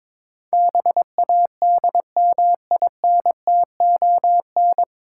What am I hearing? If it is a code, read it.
Morse code, heard as a keyed tone, about 23 wpm, 690 Hz. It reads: BADMINTON
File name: Difficult Words with Hints - 22wpm